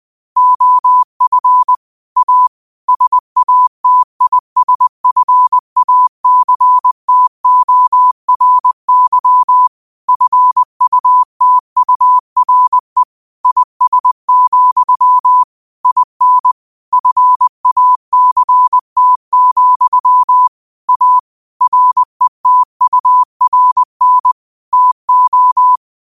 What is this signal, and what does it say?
Morse code, heard as a keyed tone, about 20 wpm, 1000 Hz. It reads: OF A SATISFACTORY FUTURE IS, IN FACT, A RETURN TO
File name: qotdfiles / 20wpm